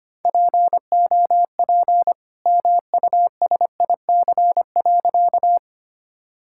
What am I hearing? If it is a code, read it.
Morse code, heard as a keyed tone, about 25 wpm, 700 Hz. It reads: POP MUSIC.
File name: practice / 25wpm / All-Age Appropriate Jokes - 25wpm